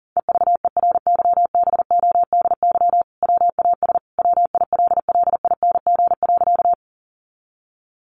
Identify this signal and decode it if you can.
Morse code, heard as a keyed tone, about 40 wpm, 700 Hz. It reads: EVERYBODY WAS WILLING.
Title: Sentences from Top 1400 Words 40wpm